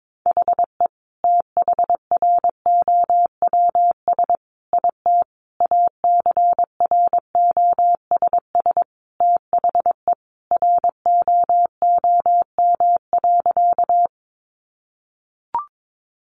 Morse code, heard as a keyed tone, about 22 wpm, 700 Hz. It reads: HE THROWS IT ACROSS THE ROOM.
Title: Additional Easy Phrases 22wpm